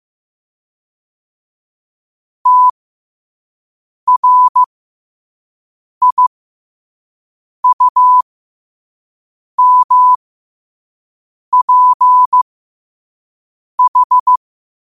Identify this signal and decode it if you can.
Morse code, heard as a keyed tone, about 15 wpm, spaced as for 5 wpm, 1000 Hz. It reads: TRIUMPH
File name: qotdfiles / 5wpm